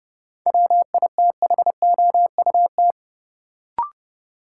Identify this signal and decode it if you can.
Morse code, heard as a keyed tone, about 30 wpm, 700 Hz. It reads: WITHOUT
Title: Sentences from Top 600 Words - New Word Drill - 30wpm